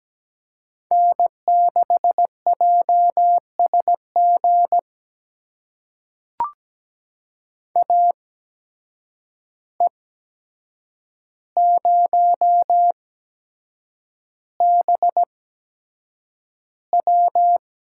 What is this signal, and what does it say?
Morse code, heard as a keyed tone, about 17 wpm, spaced as for 7 wpm, 700 Hz. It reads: N6JSG AE0BW
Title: USA Call Signs - 5 Characters - 8x Character Spacing - 17wpm